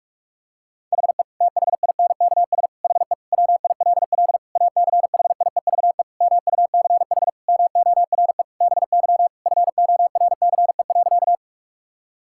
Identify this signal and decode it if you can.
Morse code, heard as a keyed tone, about 46 wpm, 700 Hz. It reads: HE THINKS HE WILL ACHIEVE MUCH MORE BY FORCE.
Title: Sentences from Top 1000 Words - Review Part 6 - 45wpm